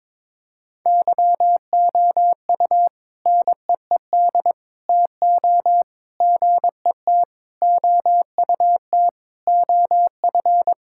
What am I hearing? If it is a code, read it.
Morse code, heard as a keyed tone, about 22 wpm, 700 Hz. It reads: YOU NEED TO GET OUT OF
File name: practice / 22wpm / Sentences with 3-Word Phrases - 22wpm